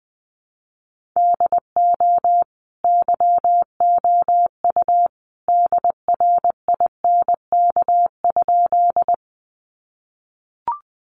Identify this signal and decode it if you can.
Morse code, heard as a keyed tone, about 20 wpm, 700 Hz. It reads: DO YOU DRINK?
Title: Sentences from Top 1500 Words 20wpm